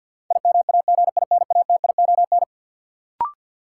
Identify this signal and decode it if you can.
Morse code, heard as a keyed tone, about 50 wpm, 700 Hz. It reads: IMAGINATION
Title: Sentences from Top 1300 Words - New Word Drill - 50wpm